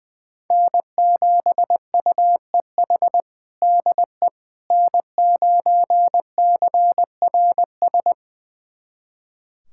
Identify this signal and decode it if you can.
Morse code, heard as a keyed tone, about 21 wpm, 700 Hz. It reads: N7UEH DE N9CRS
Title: USA Callsign DE Callsign - 5 Characters - 20wpm